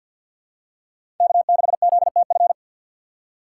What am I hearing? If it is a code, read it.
Morse code, heard as a keyed tone, about 51 wpm, 700 Hz. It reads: K6ZTF
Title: USA Call Signs - 5 Characters - 4x Character Spacing - 50wpm